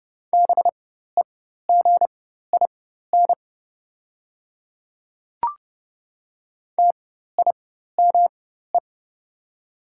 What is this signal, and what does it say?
Morse code, heard as a keyed tone, about 30 wpm, spaced as for 14 wpm, 700 Hz. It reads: BEGIN TIME
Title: 4x Character Spacing - Top 500 Words - 30wpm